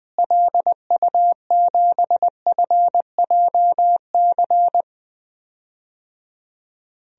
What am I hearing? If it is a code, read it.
Morse code, heard as a keyed tone, about 20 wpm, 700 Hz. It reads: LU7FJC
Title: DXCC-100 Call Signs - 6 Characters - Repeated 3x - 20wpm